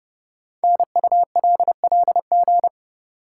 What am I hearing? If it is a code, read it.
Morse code, heard as a keyed tone, about 30 wpm, 700 Hz. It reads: NULLG
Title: Code Groups - 5 Letter:Numbers - 30wpm